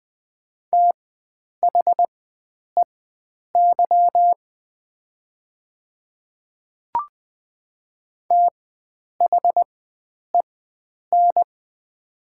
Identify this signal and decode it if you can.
Morse code, heard as a keyed tone, about 20 wpm, spaced as for 9 wpm, 700 Hz. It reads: THEY THEN
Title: 4x Character Spacing - Top 100 Words - 20wpm